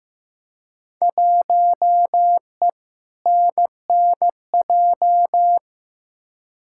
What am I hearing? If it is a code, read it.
Morse code, heard as a keyed tone, about 15 wpm, 700 Hz. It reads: 1E NNJ
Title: Field Day Exchanges - 15wpm